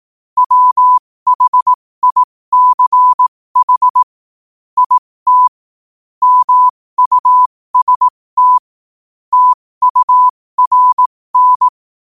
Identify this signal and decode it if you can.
Morse code, heard as a keyed tone, about 18 wpm, spaced as for 15 wpm, 1000 Hz. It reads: WHICH IT MUST TURN
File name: qotdfiles / 15wpm